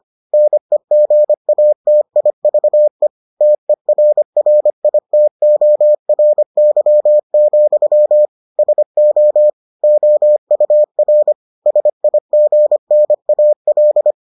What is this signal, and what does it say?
Morse code, heard as a keyed tone, about 25 wpm, 600 Hz. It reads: NEGATIVE TERRITORY, SO OUR SIGNAL